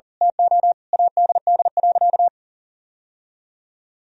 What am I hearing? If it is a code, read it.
Morse code, heard as a keyed tone, about 40 wpm, 700 Hz. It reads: TO ADD.
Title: Sentences from Top 1000 Words - Review Part 1 - 40wpm